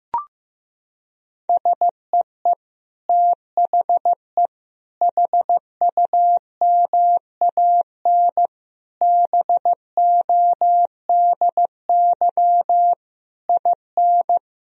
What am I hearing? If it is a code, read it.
Morse code, heard as a keyed tone, about 15 wpm, 700 Hz. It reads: SEE THE HUMAN BODY IN
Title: Sentences from Top 1000 Words - Review Part 4 - 15wpm